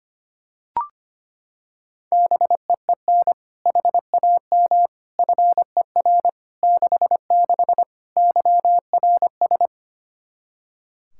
Morse code, heard as a keyed tone, about 25 wpm, 700 Hz. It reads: BEEN HAM FER 66 YRS
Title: Practice QSO - Exchange 3 - More Info - 25wpm